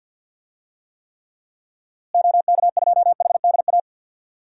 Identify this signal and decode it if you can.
Morse code, heard as a keyed tone, about 52 wpm, 700 Hz. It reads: OK2HDA
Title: DXCC-100 Call Signs - 6 Characters - Repeated 3x - 50wpm